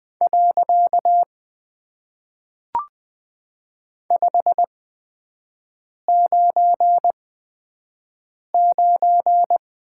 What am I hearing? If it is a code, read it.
Morse code, heard as a keyed tone, about 20 wpm, 700 Hz. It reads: . 5 9 9
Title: Top 90 QSO Elements - 8x Character Spacing - 20wpm